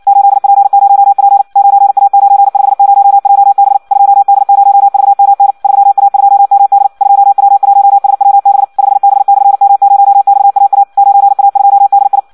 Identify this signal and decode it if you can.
Morse code, heard as a keyed tone, about 56 wpm, 800 Hz. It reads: 8P0P 9A950JB 2D0VMN 3A2MD 2W1SWB 5B/M0XAA 9A2DI
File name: Audio practica CW / qrq-tx-58 2